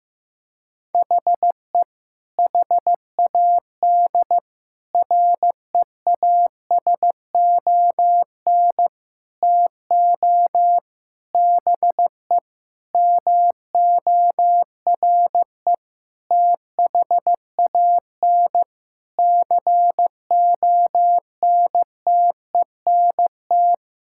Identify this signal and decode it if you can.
Morse code, heard as a keyed tone, about 15 wpm, 700 Hz. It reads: HE HAD REASON TO BE MORE THAN CONTENT
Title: Sentences from Top 1000 Words - Review Part 6 - 15wpm